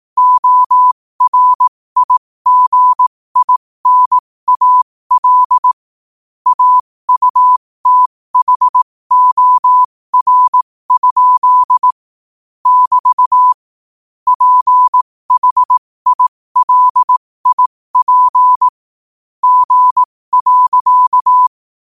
Morse code, heard as a keyed tone, about 18 wpm, spaced as for 15 wpm, 1000 Hz. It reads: ORIGINAL AUTHOR? <BT> PHILIP G.
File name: qotdfiles / 15wpm